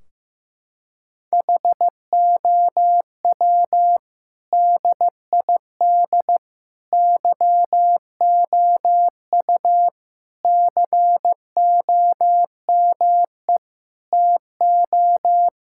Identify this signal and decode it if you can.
Morse code, heard as a keyed tone, about 15 wpm, 700 Hz. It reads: HOW DID YOU COME TO
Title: Sentences from Top 1600 Words 15wpm